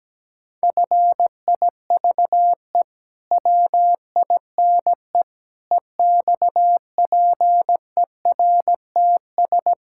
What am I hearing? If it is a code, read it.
Morse code, heard as a keyed tone, about 17 wpm, 700 Hz. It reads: FIVE WINE EXPERTS
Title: Pangram Challenge v2 - 17wpm